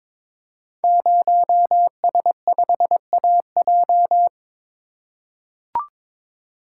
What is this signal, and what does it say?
Morse code, heard as a keyed tone, about 22 wpm, 700 Hz. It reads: 0S5AJ
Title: Code Groups - 5 Letter:Numbers - 22wpm